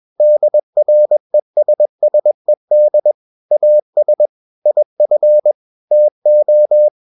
Morse code, heard as a keyed tone, about 21 wpm, 600 Hz. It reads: DRESSED AS IF TO
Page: CW med Gnister